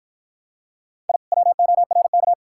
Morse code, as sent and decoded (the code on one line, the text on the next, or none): .. / .-- --- .-. -.-
I WORK